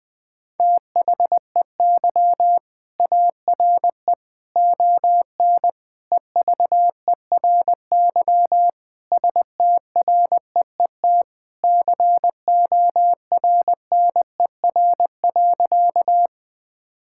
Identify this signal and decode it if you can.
Morse code, heard as a keyed tone, about 20 wpm, 700 Hz. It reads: THEY ARE ON EVERY STREET CORNER.
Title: Sentences from Top 1600 Words 20wpm